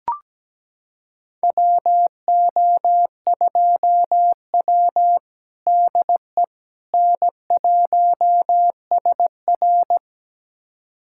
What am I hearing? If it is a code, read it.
Morse code, heard as a keyed tone, about 17 wpm, 700 Hz. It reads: WO2W DE N1SR
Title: USA Callsign DE Callsign - 4 Characters - 17wpm